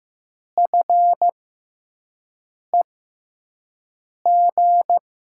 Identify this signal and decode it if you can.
Morse code, heard as a keyed tone, about 15 wpm, spaced as for 5 wpm, 700 Hz. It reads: FEG
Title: USA Call Signs - 5 Characters - 6x Character Spacing - 15wpm